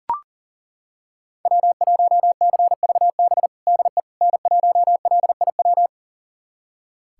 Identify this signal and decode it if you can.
Morse code, heard as a keyed tone, about 40 wpm, 700 Hz. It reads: W1CVB DE N1LIW
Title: USA Callsign DE Callsign - 5 Characters - 40wpm